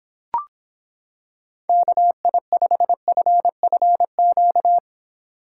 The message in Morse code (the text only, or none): KI5FFQ